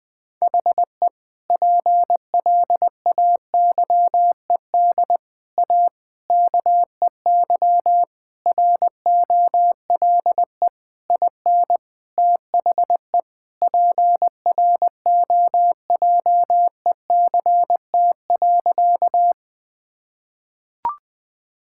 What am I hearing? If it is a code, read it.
Morse code, heard as a keyed tone, about 20 wpm, 700 Hz. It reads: HE PLAYED A KEY ROLE IN THE PROJECT.
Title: Sentences with 6-Word Phrases - 20wpm